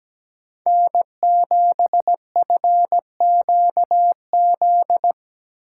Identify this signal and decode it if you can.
Morse code, heard as a keyed tone, about 17 wpm, 700 Hz. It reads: N7FQZ
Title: USA Call Signs - 8x Character Spacing - 17wpm